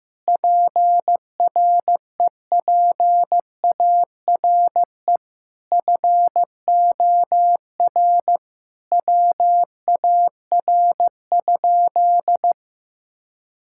Morse code, heard as a keyed tone, about 16 wpm, 700 Hz. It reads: PREPARE FOR WAR?
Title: Sentences from Top 1600 Words 15wpm